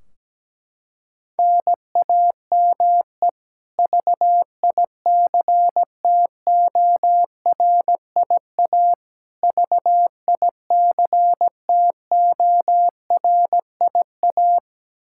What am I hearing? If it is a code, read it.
Morse code, heard as a keyed tone, about 17 wpm, 700 Hz. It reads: NAME VICTORIA VICTORIA
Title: K1USN SST QSO Practice 17wpm